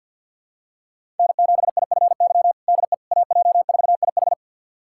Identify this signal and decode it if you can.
Morse code, heard as a keyed tone, about 51 wpm, 700 Hz. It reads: N7IFY DE AJ4IH
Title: USA Callsign DE Callsign - 5 Characters - 50wpm